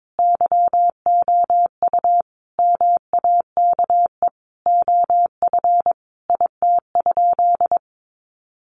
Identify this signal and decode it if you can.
Morse code, heard as a keyed tone, about 22 wpm, 700 Hz. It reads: YOU MAKE OF IT?
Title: Sentences from Top 100 Words 22wpm